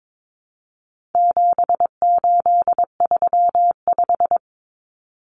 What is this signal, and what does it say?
Morse code, heard as a keyed tone, about 22 wpm, 700 Hz. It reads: 7835
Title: Numbers - 4-Digit 22wpm